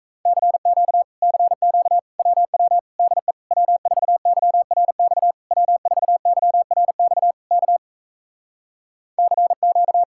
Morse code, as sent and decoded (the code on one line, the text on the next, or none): -.-. --.- / -.-. --.- / .-- .-- / -.. / .-- ....- -.-- .-. -..- / .-- ....- -.-- .-. -..- / -.- / -.-. --.-
CQ CQ WW DE W4YRX W4YRX K CQ